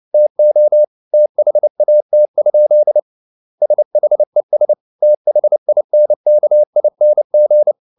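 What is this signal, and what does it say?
Morse code, heard as a keyed tone, about 29 wpm, 600 Hz. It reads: TO THAT? SHES THINKING
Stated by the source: Morse code at 29 wpm, sending Ken Follett's novel Never in English, from CW med Gnister.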